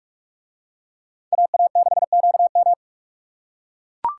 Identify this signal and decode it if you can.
Morse code, heard as a keyed tone, about 44 wpm, 700 Hz. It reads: AA6QM